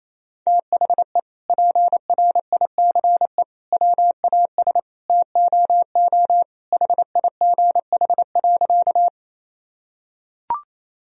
Morse code, heard as a keyed tone, about 28 wpm, 700 Hz. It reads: THE PRICE WAS TOO HIGH.
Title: Sentences from Top 1000 Words - Review Part 4 - 28wpm